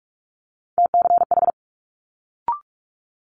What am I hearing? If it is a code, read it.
Morse code, heard as a keyed tone, about 46 wpm, 700 Hz. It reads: TCH
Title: Multi-letter Phonemes Quick Fire 45wpm